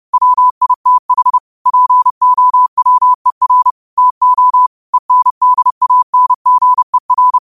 Morse code, heard as a keyed tone, about 30 wpm, 1000 Hz. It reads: WITH POWER TO ENDANGER
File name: qotdfiles / 30wpm